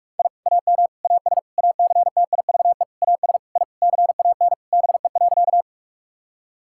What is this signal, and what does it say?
Morse code, heard as a keyed tone, about 41 wpm, 700 Hz. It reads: I AM AS ACTIVE AS I CAN BE.
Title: Sentences from Top 1000 Words - Review Part 6 - 45wpm